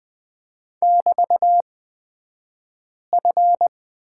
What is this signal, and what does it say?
Morse code, heard as a keyed tone, about 20 wpm, spaced as for 5 wpm, 700 Hz. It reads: <BT>F
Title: 223 Lesson 36d - BT - Two Characters 20wpm